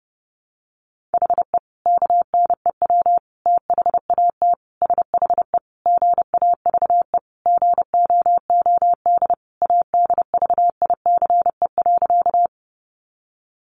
Morse code, heard as a keyed tone, about 30 wpm, 700 Hz. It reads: HE KNEW THAT SHE GAVE GOOD ADVICE.
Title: Sentences from Top 1100 Words 30wpm